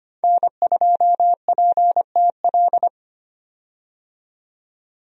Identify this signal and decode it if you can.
Morse code, heard as a keyed tone, about 25 wpm, 700 Hz. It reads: N2PTL
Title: USA Call Signs - 5 Characters - 25wpm